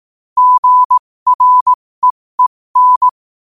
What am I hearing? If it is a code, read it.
Morse code, heard as a keyed tone, about 18 wpm, spaced as for 15 wpm, 1000 Hz. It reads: GREEN
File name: qotdfiles / 15wpm